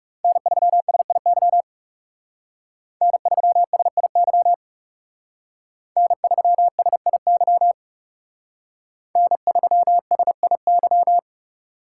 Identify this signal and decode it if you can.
Morse code, heard as a keyed tone, about 38 wpm, 700 Hz. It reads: N3SIY N3SIY N3SIY N3SIY